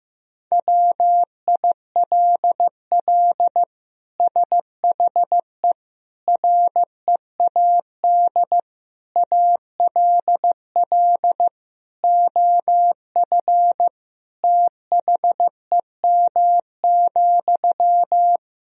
Morse code, heard as a keyed tone, about 15 wpm, 700 Hz. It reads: WILL SHE READ ALL OF THEM,
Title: Sentences from Top 1000 Words - Review Part 4 - 15wpm